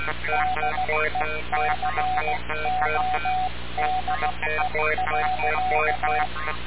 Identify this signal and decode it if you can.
Morse code, heard as a keyed tone, about 16 wpm, 740 Hz, with noise. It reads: DECODE.
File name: radiocap / REC z